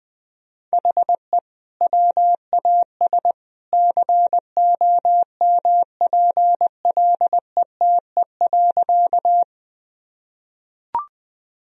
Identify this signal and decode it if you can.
Morse code, heard as a keyed tone, about 20 wpm, 700 Hz. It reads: HE WAS COMPLETE.
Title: Sentences from Top 1000 Words - Review Part 2 - 20wpm